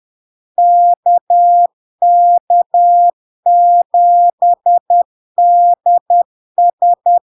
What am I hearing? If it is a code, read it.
Morse code, heard as a keyed tone, about 10 wpm, 700 Hz. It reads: KK7DS
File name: LARK-FD-2022-Log-in-CW-10wpm-700-Hz